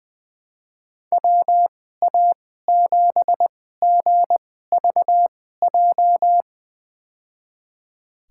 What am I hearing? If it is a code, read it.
Morse code, heard as a keyed tone, about 20 wpm, spaced as for 14 wpm, 700 Hz. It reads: WA7GVJ
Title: USA Call Signs - 6 Characters - 2x Character Spacing - 20wpm